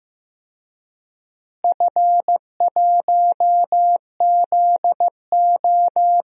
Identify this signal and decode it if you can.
Morse code, heard as a keyed tone, about 15 wpm, 700 Hz. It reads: F1ZO
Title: DXCC-100 Call Signs - 4 Characters - Repeated 2x - 15wpm